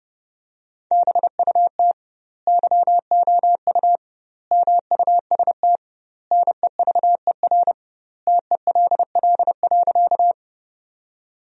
Morse code, heard as a keyed tone, about 30 wpm, 700 Hz. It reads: BUT YOU MUST NEVER TELL.
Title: 2x Word-Spacing - Sentences from Top 300 Words - 30wpm